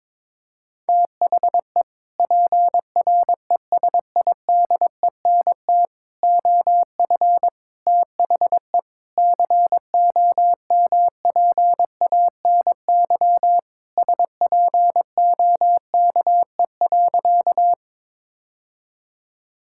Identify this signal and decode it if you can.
Morse code, heard as a keyed tone, about 22 wpm, 700 Hz. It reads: THE PRESIDENT OF THE COMPANY SPOKE.
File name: practice / 22wpm / Sentences with 3-Word Phrases - 22wpm